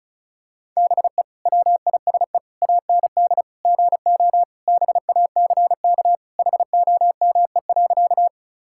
Morse code, heard as a keyed tone, about 34 wpm, 700 Hz. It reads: BE WISE AND GO BACK HOME.
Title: Sentences from Top 1600 Words 35wpm